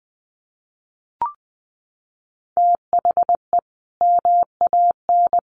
Morse code, heard as a keyed tone, about 20 wpm, 700 Hz. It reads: THE MAN
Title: A Sound Of Thunder 20wpm